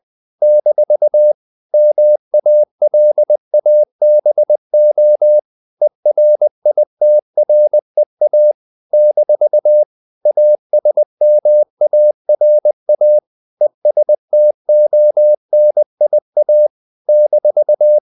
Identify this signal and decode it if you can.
Morse code, heard as a keyed tone, about 20 wpm, 600 Hz. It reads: - MALABO ERITREA - ASMARA ESTONIA -
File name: CW / Practice / NationalCapitals20